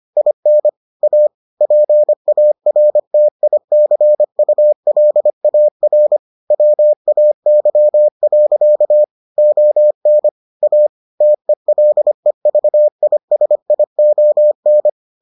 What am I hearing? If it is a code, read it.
Morse code, heard as a keyed tone, about 25 wpm, 600 Hz. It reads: IN A PARTICULAR WAY. ON A TELEVISION